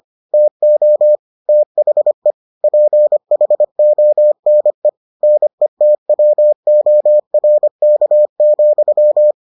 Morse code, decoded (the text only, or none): TO THE PHONE NETWORK,